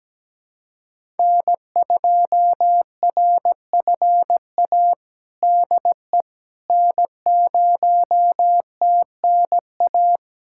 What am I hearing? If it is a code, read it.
Morse code, heard as a keyed tone, about 17 wpm, 700 Hz. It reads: N2RFA DE N0TNA
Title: USA Callsign DE Callsign - 5 Characters - 17wpm